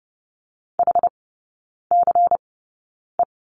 Morse code, heard as a keyed tone, about 30 wpm, spaced as for 9 wpm, 700 Hz. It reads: HCE
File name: Lesson 16e - Letter c - Three Characters 30wpm